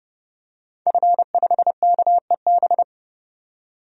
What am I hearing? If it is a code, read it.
Morse code, heard as a keyed tone, about 30 wpm, 700 Hz. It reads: F5KEB
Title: DXCC-100 Call Signs - 5 Characters - 30wpm